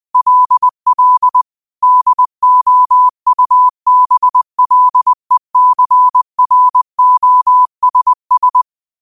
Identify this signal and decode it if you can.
Morse code, heard as a keyed tone, about 20 wpm, 1000 Hz. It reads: LL DOUBLECROSS
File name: qotdfiles / 20wpm